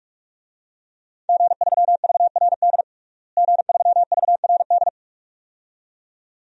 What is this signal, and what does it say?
Morse code, heard as a keyed tone, about 42 wpm, 700 Hz. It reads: G3VRD G3VRD
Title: DXCC-100 Call Signs - 5 Characters - Repeated 2x - 45wpm